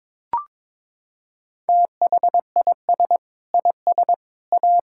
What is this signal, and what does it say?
Morse code, heard as a keyed tone, about 22 wpm, 700 Hz. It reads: THIS IS A
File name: practice / 22wpm / Sentences with 3-Word Phrases - 22wpm